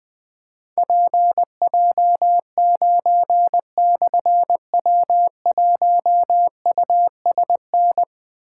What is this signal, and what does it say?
Morse code, heard as a keyed tone, about 21 wpm, 700 Hz. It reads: PJ9/W1USN
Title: Difficult Call Signs 20wpm